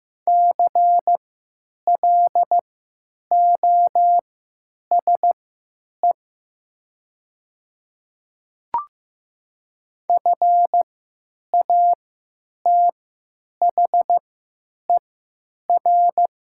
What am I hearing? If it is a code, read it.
Morse code, heard as a keyed tone, about 15 wpm, spaced as for 9 wpm, 700 Hz. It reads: CLOSE FATHER
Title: 3x Character Spacing - Top 500 Words - 15wpm